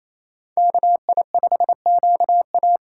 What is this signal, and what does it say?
Morse code, heard as a keyed tone, about 32 wpm, 700 Hz. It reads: KI5QA
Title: Speed Racing - USA Call Signs 28wpm